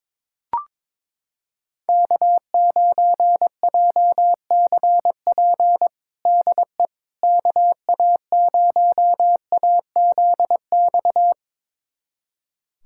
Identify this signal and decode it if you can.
Morse code, heard as a keyed tone, about 22 wpm, 700 Hz. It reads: K9JCP DE KA0AZX
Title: USA Callsign DE Callsign 22wpm